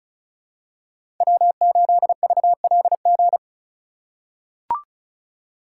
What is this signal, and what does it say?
Morse code, heard as a keyed tone, about 34 wpm, 700 Hz. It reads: W8VLG
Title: USA Call Signs - 5 Characters - 4x Character Spacing - 35wpm